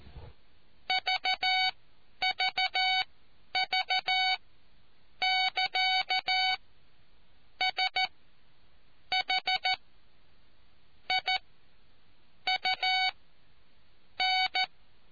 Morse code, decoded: VVV <KA> SH I U N

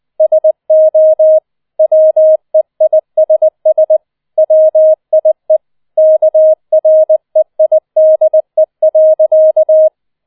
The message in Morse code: SO WEISS WIE KREIDE.

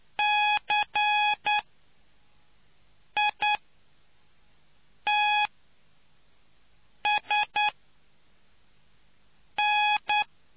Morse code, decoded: CITSN

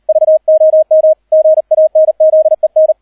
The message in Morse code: VOM GANZEN

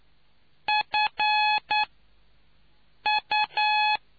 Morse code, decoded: FU

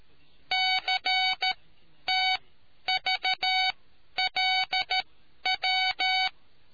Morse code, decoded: CTVLW

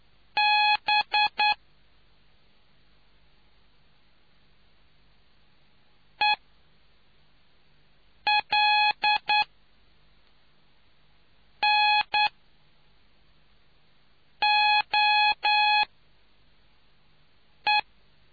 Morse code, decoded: B ELNOE

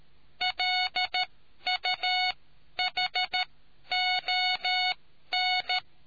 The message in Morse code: LUHON

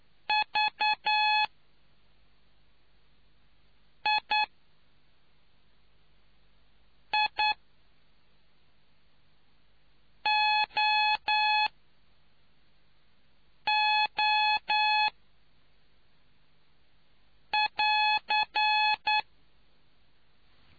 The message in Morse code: VIIOO<AR>

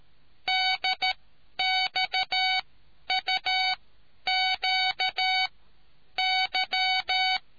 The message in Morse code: DXUQY

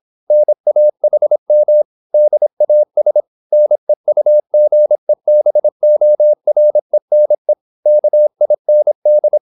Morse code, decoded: NAHM DAS NEUGEBORENE KIND

